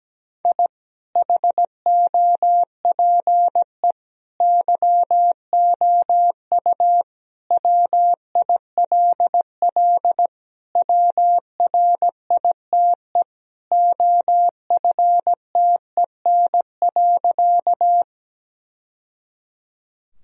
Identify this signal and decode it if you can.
Morse code, heard as a keyed tone, about 17 wpm, 700 Hz. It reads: I HOPE YOU WILL WRITE OFTEN.
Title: Sentences from Top 1000 Words - Review Part 1 - 17wpm